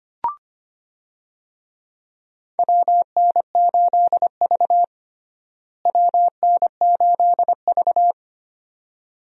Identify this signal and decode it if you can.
Morse code, heard as a keyed tone, about 25 wpm, 700 Hz. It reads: WN8V WN8V
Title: USA Call Signs - 4 Characters - Repeated 2x - 25wpm